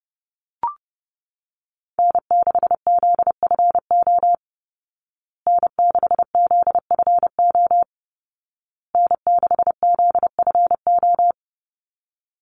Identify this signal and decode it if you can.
Morse code, heard as a keyed tone, about 30 wpm, 700 Hz. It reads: N6ZFO N6ZFO N6ZFO
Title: Call Signs Repeated 2x - 30wpm